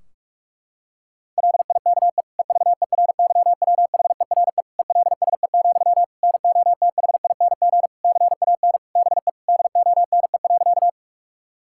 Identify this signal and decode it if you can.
Morse code, heard as a keyed tone, about 41 wpm, 700 Hz. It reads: LIKE EVERYWHERE ELSE, NOTHING CAN BE DONE.
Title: Sentences from Top 1600 Words 45wpm